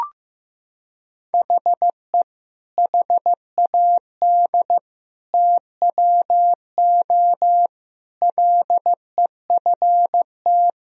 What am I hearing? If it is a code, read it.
Morse code, heard as a keyed tone, about 15 wpm, 700 Hz. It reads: HE HAD TWO LEFT
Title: Sentences from Top 1000 Words - Review Part 1 - 15wpm